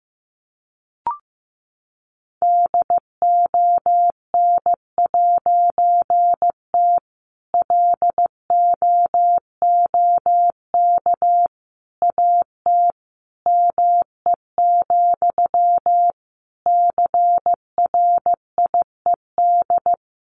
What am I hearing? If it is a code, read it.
Morse code, heard as a keyed tone, about 15 wpm, 700 Hz. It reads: DON'T LOOK AT ME, CRIED
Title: A Sound Of Thunder 15wpm